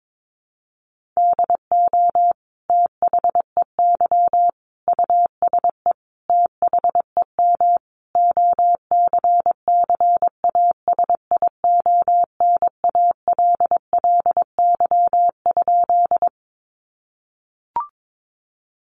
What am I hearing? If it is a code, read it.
Morse code, heard as a keyed tone, about 22 wpm, 700 Hz. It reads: DO THEY USE THEM OCCASIONALLY?
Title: Sentences from Top 1300 Words 22wpm